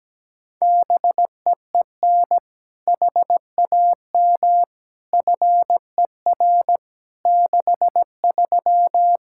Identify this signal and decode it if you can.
Morse code, heard as a keyed tone, about 17 wpm, 700 Hz. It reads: BEEN HAM FER 63
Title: Practice QSO - Exchange 3 - More Info - 17wpm